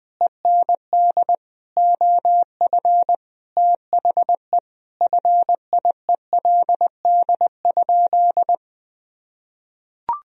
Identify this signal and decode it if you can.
Morse code, heard as a keyed tone, about 20 wpm, 700 Hz. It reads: END OF THE FIELD?
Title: Sentences from Top 1000 Words - Review Part 2 - 20wpm